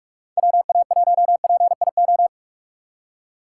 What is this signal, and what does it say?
Morse code, heard as a keyed tone, about 59 wpm, 700 Hz. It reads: WA1PIO